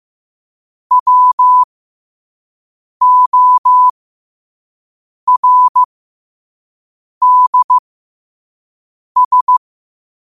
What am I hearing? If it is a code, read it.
Morse code, heard as a keyed tone, about 15 wpm, spaced as for 5 wpm, 1000 Hz. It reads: WORDS
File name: qotdfiles / 5wpm